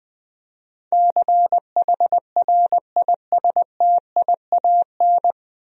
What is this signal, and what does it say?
Morse code, heard as a keyed tone, about 20 wpm, 700 Hz. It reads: CHRISTIAN
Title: USA Common Names - 20wpm